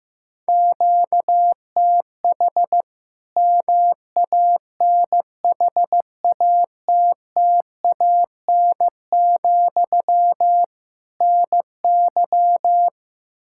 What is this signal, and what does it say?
Morse code, heard as a keyed tone, about 16 wpm, 700 Hz. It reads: QTH MANHATTAN, NY
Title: Practice QSO - Exchange 1 - Beginning - 15wpm